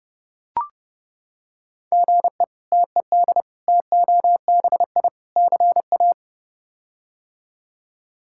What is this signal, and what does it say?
Morse code, heard as a keyed tone, about 31 wpm, 700 Hz. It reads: GE TED TOBI CA
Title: K1USN SST QSO Practice - 30wpm